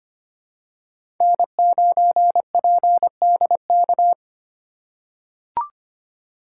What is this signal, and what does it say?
Morse code, heard as a keyed tone, about 25 wpm, 700 Hz. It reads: N9PDK
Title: USA Call Signs - 5 Characters - 4x Character Spacing - 25wpm